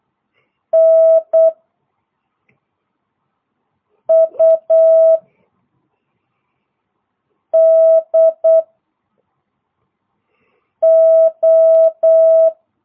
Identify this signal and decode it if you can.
Morse code, heard as a keyed tone, about 8 wpm, 650 Hz. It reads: N U D O